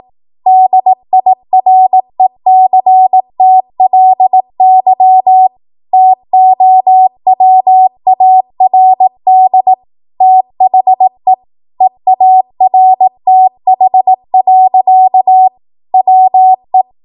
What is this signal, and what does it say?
Morse code, heard as a keyed tone, about 18 wpm, 750 Hz. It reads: DIRECTLY TOWARD THE EARTH. WE
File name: Archive / 18 WPM / 18WPM